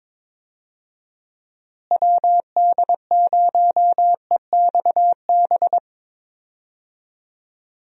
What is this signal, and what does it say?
Morse code, heard as a keyed tone, about 22 wpm, 700 Hz. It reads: WD0EXB
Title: USA Call Signs - 6 Characters - Repeated 2x - 22wpm